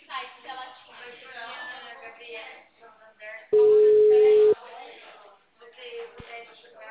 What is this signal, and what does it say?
Ringback tone (Europe, 425 Hz), with speech.